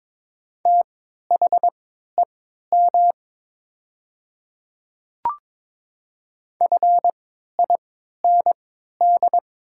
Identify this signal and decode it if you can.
Morse code, heard as a keyed tone, about 22 wpm, spaced as for 13 wpm, 700 Hz. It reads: THEM FIND
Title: 3x Character Spacing - Top 200 Words - 22wpm